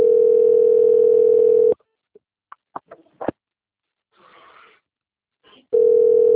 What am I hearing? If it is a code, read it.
Ringback tone (North America), 440 + 480 Hz.
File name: out- - - - - .4362